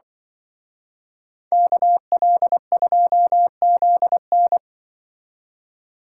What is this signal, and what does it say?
Morse code, heard as a keyed tone, about 25 wpm, 700 Hz. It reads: KL2ZN
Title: Speed Racing - USA 2x2 Call Signs 20wpm